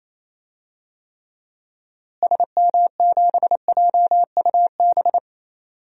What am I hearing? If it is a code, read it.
Morse code, heard as a keyed tone, about 28 wpm, 700 Hz. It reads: SM7JUB
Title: DXCC-100 Call Signs - 6 Characters - Repeated 2x - 28wpm